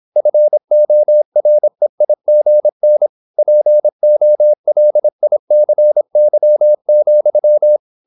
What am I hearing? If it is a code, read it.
Morse code, heard as a keyed tone, about 26 wpm, 600 Hz. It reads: FOREIGN POLICY,
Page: Never 26wpm | CW med Gnister